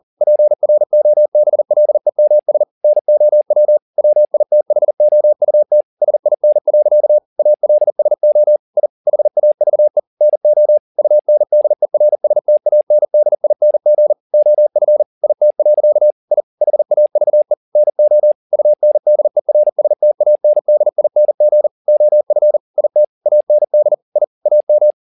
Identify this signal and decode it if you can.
Morse code, heard as a keyed tone, about 40 wpm, 600 Hz. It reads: PROBLEMS NOW WITHOUT SIN. ALSO I HAVE NO UNDERSTANDING OF IT. I HAVE NO UNDERSTANDING OF IT AND I AM
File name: the old man and the sea 2 40wpm